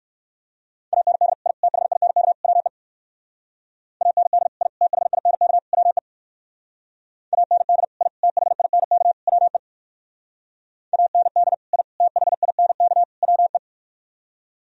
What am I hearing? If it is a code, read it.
Morse code, heard as a keyed tone, about 58 wpm, 700 Hz. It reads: AND I THINK WE AND I THINK WE AND I THINK WE AND I THINK WE